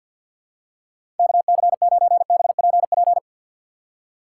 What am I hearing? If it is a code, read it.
Morse code, heard as a keyed tone, about 47 wpm, 700 Hz. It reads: KC9BPP
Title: USA Call Signs - 6 Characters - 6x Character Spacing - 50wpm